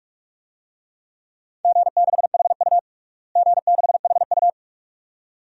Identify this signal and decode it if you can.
Morse code, heard as a keyed tone, about 46 wpm, 700 Hz. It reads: G6HU G6HU
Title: DXCC-100 Call Signs - 4 Characters - Repeated 2x - 45wpm